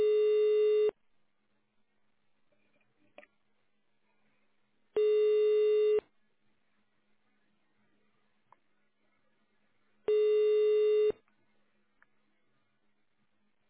Ringback tone (Europe, 425 Hz).